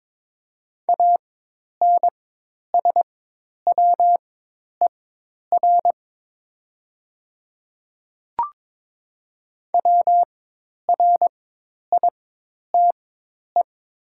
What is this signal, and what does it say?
Morse code, heard as a keyed tone, about 22 wpm, spaced as for 10 wpm, 700 Hz. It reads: ANSWER WRITE
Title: 4x Character Spacing - Top 300 Words - 22wpm